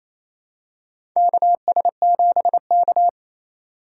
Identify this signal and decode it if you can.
Morse code, heard as a keyed tone, about 28 wpm, 700 Hz. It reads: KS7K
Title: USA Call Signs - 4x Character Spacing - 28wpm